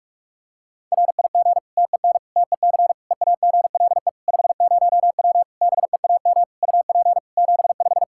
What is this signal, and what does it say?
Morse code, heard as a keyed tone, about 46 wpm, 700 Hz. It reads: RIG TEN TEC EAGLE 50W BEAM UP 75
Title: Practice QSO - Exchange 2 - Info - 45wpm